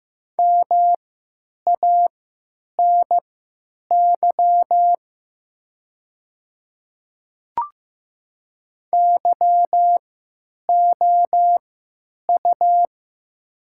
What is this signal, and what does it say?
Morse code, heard as a keyed tone, about 15 wpm, spaced as for 9 wpm, 700 Hz. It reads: MANY YOU
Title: 3x Character Spacing - Top 100 Words - 15wpm